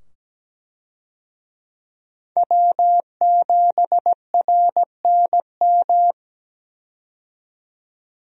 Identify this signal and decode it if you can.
Morse code, heard as a keyed tone, about 17 wpm, 700 Hz. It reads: W7RNM
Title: USA Call Signs - 5 Characters - 17wpm